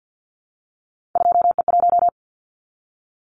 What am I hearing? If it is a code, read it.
Morse code, heard as a keyed tone, about 50 wpm, 700 Hz. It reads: 2E1
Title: Callsign Prefixes for Top 25 DXCC - Repeated 3x - 50wpm